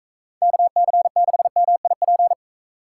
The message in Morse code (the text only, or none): KC6MIP